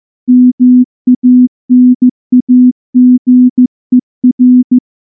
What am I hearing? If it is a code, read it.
Morse code, heard as a keyed tone, about 14 wpm, 250 Hz. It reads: MANAGER
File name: morse-training-15WPM